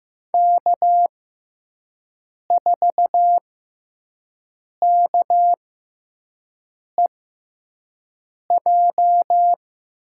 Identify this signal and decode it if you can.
Morse code, heard as a keyed tone, about 15 wpm, spaced as for 5 wpm, 700 Hz. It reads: K4KEJ
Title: USA Call Signs - 5 Characters - 6x Character Spacing - 15wpm